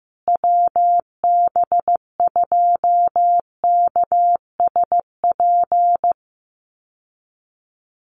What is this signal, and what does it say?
Morse code, heard as a keyed tone, about 15 wpm, 700 Hz. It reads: WB2KSP
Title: USA Call Signs Repeated 3x 15wpm